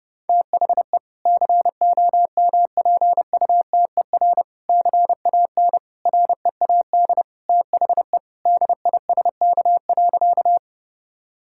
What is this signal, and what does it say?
Morse code, heard as a keyed tone, about 32 wpm, 700 Hz. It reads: THE COMPUTER CAN READ THE DISK.
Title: Sentences from Top 1000 Words - Review Part 6 - 30wpm